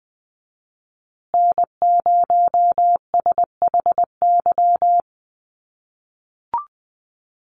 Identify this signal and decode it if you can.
Morse code, heard as a keyed tone, about 20 wpm, 700 Hz. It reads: N0SHY
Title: USA Call Signs Repeated 2x 20wpm